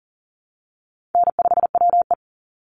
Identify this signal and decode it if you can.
Morse code, heard as a keyed tone, about 40 wpm, 700 Hz. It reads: N5WE
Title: Call Signs Repeated 2x - 40wpm